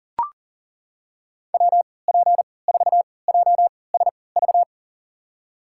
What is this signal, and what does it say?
Morse code, heard as a keyed tone, about 40 wpm, spaced as for 23 wpm, 700 Hz. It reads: WP4JSV
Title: USA Call Signs - 6 Characters - 3x Character Spacing - 40wpm